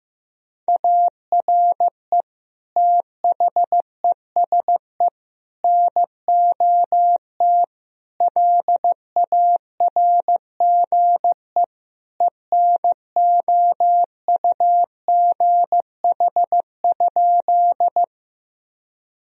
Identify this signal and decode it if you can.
Morse code, heard as a keyed tone, about 15 wpm, 700 Hz. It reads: ARE THESE NOT LARGE ENOUGH?
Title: Sentences from Top 1000 Words - Review Part 1 - 15wpm